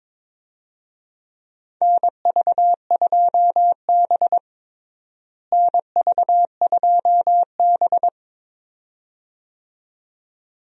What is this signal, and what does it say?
Morse code, heard as a keyed tone, about 21 wpm, 700 Hz. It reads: NV2B NV2B
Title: USA Call Signs - 4 Characters - Repeated 2x - 22wpm